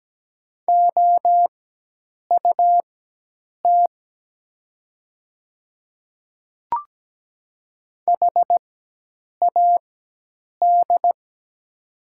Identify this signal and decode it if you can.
Morse code, heard as a keyed tone, about 17 wpm, spaced as for 8 wpm, 700 Hz. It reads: OUT HAD
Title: 4x Character Spacing - Top 200 Words - 17wpm